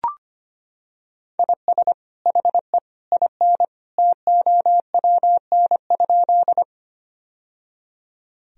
Morse code, heard as a keyed tone, about 25 wpm, 700 Hz. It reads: IS HE IN TOWN?
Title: Sentences from Top 1000 Words - Review Part 2 - 25wpm